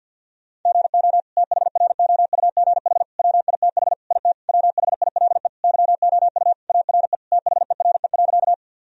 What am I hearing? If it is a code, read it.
Morse code, heard as a keyed tone, about 52 wpm, 700 Hz. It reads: GO THROUGH WITH IT WHILE YOU ARE THERE.
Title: Sentences from Top 1000 Words - Review Part 1 - 50wpm